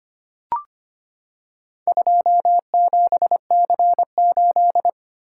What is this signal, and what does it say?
Morse code, heard as a keyed tone, about 25 wpm, 700 Hz. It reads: 27C8
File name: practice / 25wpm / Four Letter-Number Combinations - 25wpm